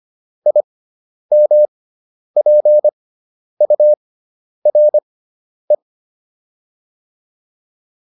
Morse code, heard as a keyed tone, about 25 wpm, spaced as for 10 wpm, 600 Hz. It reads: IMPURE